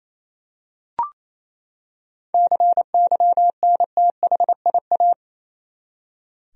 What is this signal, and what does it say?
Morse code, heard as a keyed tone, about 28 wpm, 700 Hz. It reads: CYNTHIA
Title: USA Common Names - 28wpm